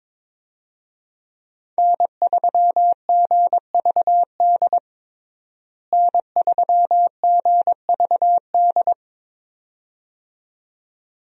Morse code, heard as a keyed tone, about 22 wpm, 700 Hz. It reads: N3GVD N3GVD
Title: USA Call Signs - 5 Characters - Repeated 2x - 22wpm